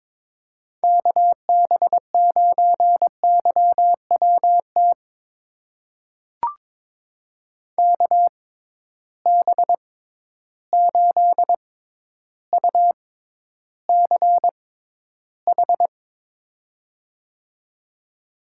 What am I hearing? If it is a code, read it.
Morse code, heard as a keyed tone, about 22 wpm, spaced as for 11 wpm, 700 Hz. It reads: KB9YWT KB8UCH
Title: USA Call Signs - 6 Characters - 6x Character Spacing - 22wpm